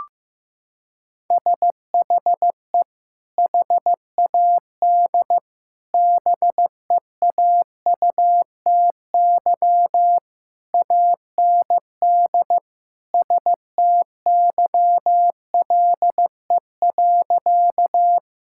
Morse code, heard as a keyed tone, about 15 wpm, 700 Hz. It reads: SHE HAD BEAUTY AND STYLE.
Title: Sentences from Top 1000 Words - Review Part 5 - 15wpm